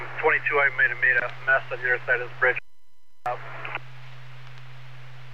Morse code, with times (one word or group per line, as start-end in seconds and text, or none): none